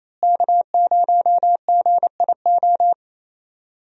Morse code, heard as a keyed tone, about 28 wpm, 700 Hz. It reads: K0GIO
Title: USA Call Signs - 5 Characters - Repeated 3x - 28wpm